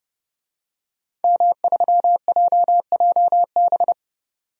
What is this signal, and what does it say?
Morse code, heard as a keyed tone, about 30 wpm, 700 Hz. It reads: M3JJB
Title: DXCC-100 Call Signs - 5 Characters - 30wpm